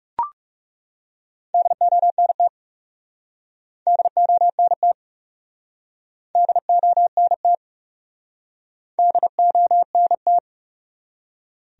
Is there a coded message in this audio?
Yes: Morse code from 1.5 to 10.4 s, heard as a keyed tone, about 43 wpm, 700 Hz. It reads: DONT DONT DONT DONT